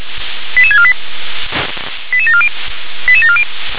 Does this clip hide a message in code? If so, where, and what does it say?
Yes: five-tone selective call (ZVEI) from 0.6 to 3.4 s, heard as quick runs of five tones, tones of about 70 ms. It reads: 88648 88740 88740